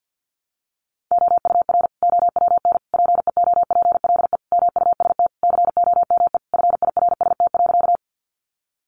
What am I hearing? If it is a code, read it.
Morse code, heard as a keyed tone, about 50 wpm, 700 Hz. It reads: OUR OWN PEOPLE MUST COME FIRST.
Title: Sentences from Top 300 Words 50wpm